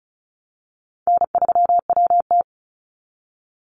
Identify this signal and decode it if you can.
Morse code, heard as a keyed tone, about 34 wpm, 700 Hz. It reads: N3WT
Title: Call Signs Repeated 3x - 35wpm